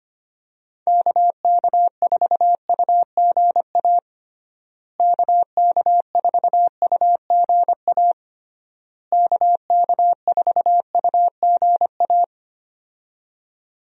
Morse code, heard as a keyed tone, about 25 wpm, 700 Hz. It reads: KK4UGA KK4UGA KK4UGA
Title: USA Call Signs - 6 Characters - Repeated 3x - 25wpm